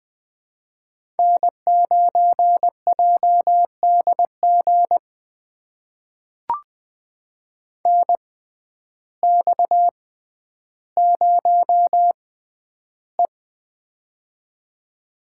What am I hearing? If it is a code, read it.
Morse code, heard as a keyed tone, about 21 wpm, spaced as for 11 wpm, 700 Hz. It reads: N9JDG NX0E
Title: USA Call Signs - 6x Character Spacing - 20wpm